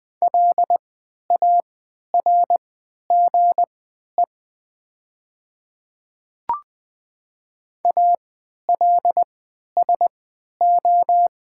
Morse code, heard as a keyed tone, about 20 wpm, spaced as for 11 wpm, 700 Hz. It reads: LARGE ALSO
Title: 3x Character Spacing - Top 200 Words - 20wpm